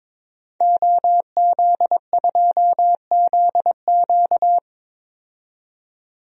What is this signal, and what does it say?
Morse code, heard as a keyed tone, about 21 wpm, 700 Hz. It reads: OZ2ZQ
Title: DXCC-100 Call Signs - 5 Characters - 22wpm